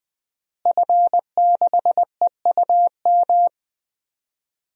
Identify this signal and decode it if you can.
Morse code, heard as a keyed tone, about 20 wpm, 700 Hz. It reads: F6EUM
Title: DXCC-100 Call Signs - 5 Characters - Repeated 2x - 20wpm